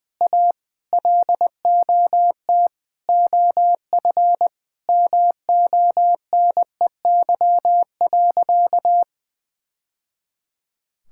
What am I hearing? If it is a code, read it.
Morse code, heard as a keyed tone, about 19 wpm, 700 Hz. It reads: A LOT OF MONEY.
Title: Sentences from Top 1000 Words - Review Part 5 - 20wpm